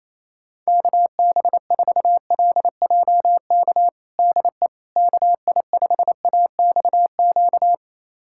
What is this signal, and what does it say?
Morse code, heard as a keyed tone, about 27 wpm, 700 Hz. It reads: KB4LJK DE KI5AXQ
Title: USA Callsign DE Callsign - 6 Characters - 28wpm